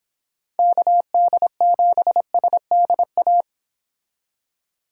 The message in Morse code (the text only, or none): KD7SDA